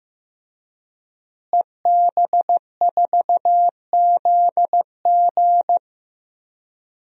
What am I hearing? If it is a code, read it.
Morse code, heard as a keyed tone, about 15 wpm, 700 Hz. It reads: EB4ZG
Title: DXCC-100 Call Signs - 5 Characters - Repeated 3x - 15wpm